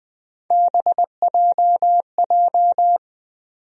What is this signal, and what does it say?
Morse code, heard as a keyed tone, about 20 wpm, 700 Hz. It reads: BJJ
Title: USA Call Sign Postfixes - 3 Letters 20wpm